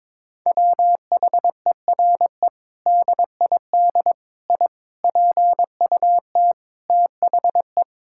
Morse code, heard as a keyed tone, about 22 wpm, 700 Hz. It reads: WHERE DID I PUT THE
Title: Sentences from Top 1600 Words 22wpm